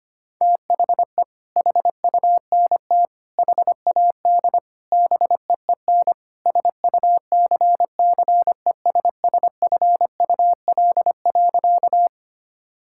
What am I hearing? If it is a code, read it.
Morse code, heard as a keyed tone, about 25 wpm, 700 Hz. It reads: THE HUNT HAD BEEN SUCCESSFUL.
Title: Sentences from Top 1600 Words 25wpm